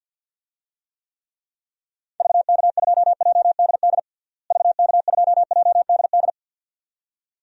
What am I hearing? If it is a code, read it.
Morse code, heard as a keyed tone, about 52 wpm, 700 Hz. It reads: VK2JDD VK2JDD
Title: DXCC-100 Call Signs - 6 Characters - Repeated 2x - 50wpm